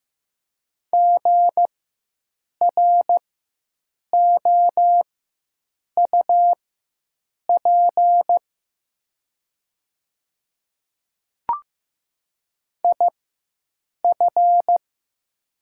Morse code, heard as a keyed tone, about 15 wpm, spaced as for 7 wpm, 700 Hz. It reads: GROUP IF
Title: 4x Character Spacing - Top 300 Words - 15wpm